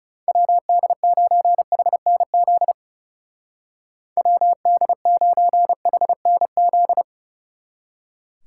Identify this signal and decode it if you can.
Morse code, heard as a keyed tone, about 37 wpm, 700 Hz. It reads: WD9HNZ WD9HNZ
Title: Speed Racing - USA 2x3 Call Signs 30wpm